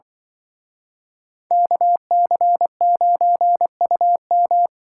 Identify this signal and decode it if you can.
Morse code, heard as a keyed tone, about 24 wpm, 700 Hz. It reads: KC9UM
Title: Speed Racing - USA 2x2 Call Signs 20wpm